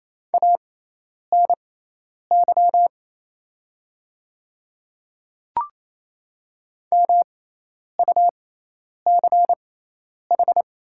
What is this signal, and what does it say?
Morse code, heard as a keyed tone, about 28 wpm, spaced as for 10 wpm, 700 Hz. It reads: ANY MUCH
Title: 6x Character Spacing - Top 300 Words - 28wpm